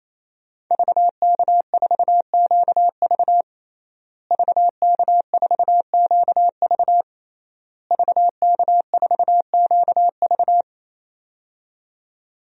Morse code, heard as a keyed tone, about 28 wpm, 700 Hz. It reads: VK4QV VK4QV VK4QV
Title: DXCC-100 Call Signs - 5 Characters - Repeated 3x - 28wpm